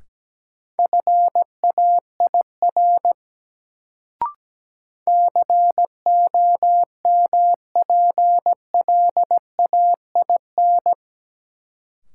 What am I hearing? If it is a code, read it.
Morse code, heard as a keyed tone, about 17 wpm, 700 Hz. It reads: FAIR COMPLAIN
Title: Sentences from Top 1300 Words - New Word Drill Alt - 17wpm